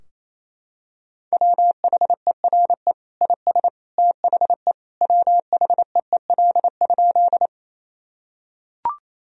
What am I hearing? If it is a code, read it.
Morse code, heard as a keyed tone, about 28 wpm, 700 Hz. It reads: WHERE IS THE WHEEL?
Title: Sentences from Top 1000 Words - Review Part 3 - 28wpm